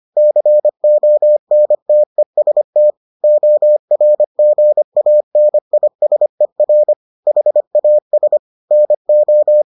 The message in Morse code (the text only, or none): CONTEST ORGANISER HAS NO